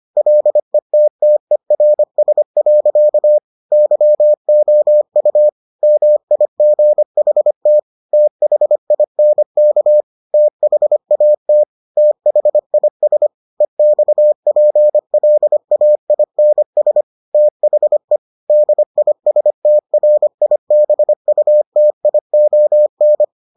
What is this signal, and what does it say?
Morse code, heard as a keyed tone, about 25 wpm, 600 Hz. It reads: LETTERS. YOU MIGHT THINK THAT THIS EXPLAINS THE DISTRIBUTION